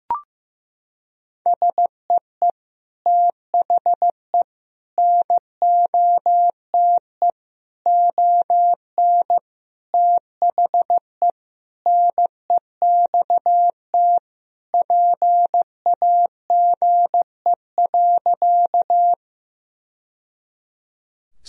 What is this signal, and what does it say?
Morse code, heard as a keyed tone, about 15 wpm, 700 Hz. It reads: SEE THE NOTE ON THE NEXT PAGE.
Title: Sentences from Top 1000 Words - Review Part 2 - 15wpm